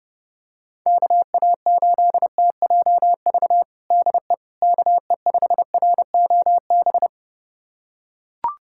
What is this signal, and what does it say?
Morse code, heard as a keyed tone, about 32 wpm, 700 Hz. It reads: KA8TJV DE KE5ROB
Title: USA Callsign DE Callsign 30wpm